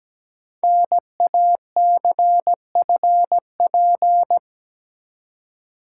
Morse code, heard as a keyed tone, about 17 wpm, 700 Hz. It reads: NACFP
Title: Code Groups - 5 Letter:Numbers - 17wpm